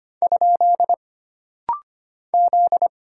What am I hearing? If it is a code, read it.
Morse code, heard as a keyed tone, about 25 wpm, 700 Hz. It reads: ? Z
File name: Single Character (Letter:Number:Punctuation:Prosign) - Rapid-Fire - 25wpm